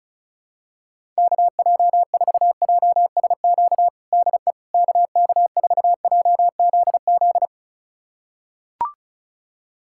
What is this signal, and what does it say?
Morse code, heard as a keyed tone, about 34 wpm, 700 Hz. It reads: KJ4JSQ DE KK4JZZ
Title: USA Callsign DE Callsign 35wpm